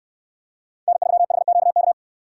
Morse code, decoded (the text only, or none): N3HQK